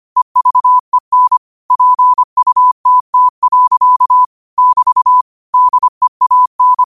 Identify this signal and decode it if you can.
Morse code, heard as a keyed tone, about 25 wpm, 1000 Hz. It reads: EVEN PUTT. <BT> DEAN
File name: qotdfiles / 25wpm